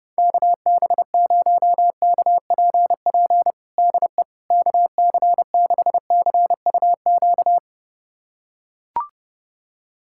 Morse code, heard as a keyed tone, about 32 wpm, 700 Hz. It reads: KB0KPP DE KC6CUQ
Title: USA Callsign DE Callsign 30wpm